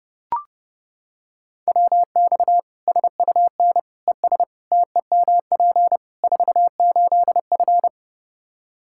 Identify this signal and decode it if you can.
Morse code, heard as a keyed tone, about 30 wpm, 700 Hz. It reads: WX SUN ES TEMP 48F
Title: K1USN SST QSO Practice 30wpm